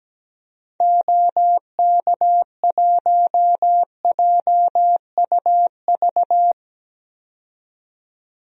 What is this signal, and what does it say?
Morse code, heard as a keyed tone, about 17 wpm, 700 Hz. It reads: OK1JUV
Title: DXCC-100 Call Signs - 6 Characters - 17wpm